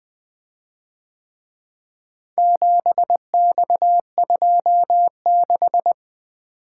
Morse code, heard as a keyed tone, about 23 wpm, 700 Hz. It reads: 7X26
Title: Speed-Racing - Four Letter-Number Combinations - 20wpm